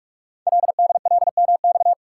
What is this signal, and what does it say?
Morse code, heard as a keyed tone, about 47 wpm, 700 Hz. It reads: LDLMX
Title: Code Groups - 5 Letters - 45wpm